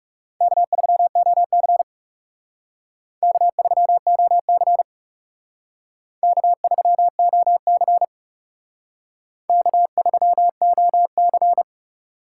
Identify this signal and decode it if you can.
Morse code, heard as a keyed tone, about 37 wpm, 700 Hz. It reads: K3OC K3OC K3OC K3OC
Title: Speed Racing - USA 1x2 Call Signs 30wpm